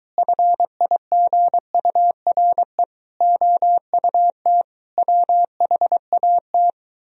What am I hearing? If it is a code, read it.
Morse code, heard as a keyed tone, about 24 wpm, 690 Hz. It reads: FIGURE OUT WHAT